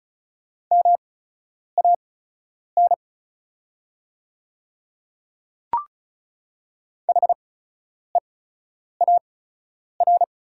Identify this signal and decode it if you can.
Morse code, heard as a keyed tone, about 36 wpm, spaced as for 10 wpm, 690 Hz. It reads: MAN HEAR
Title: 8x Character Spacing - Top 500 Words - 35wpm